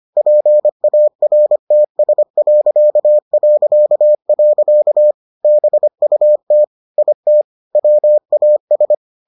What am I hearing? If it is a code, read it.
Morse code, heard as a keyed tone, about 25 wpm, 600 Hz. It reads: PARTS... BUT IT WAS